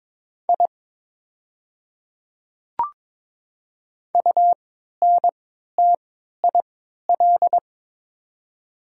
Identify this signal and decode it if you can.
Morse code, heard as a keyed tone, about 23 wpm, spaced as for 13 wpm, 700 Hz. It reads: I UNTIL
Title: 3x Character Spacing - Top 300 Words - 22wpm